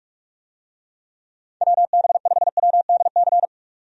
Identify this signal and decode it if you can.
Morse code, heard as a keyed tone, about 46 wpm, 700 Hz. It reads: WB5WDC
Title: USA Call Signs - 6 Characters - 45wpm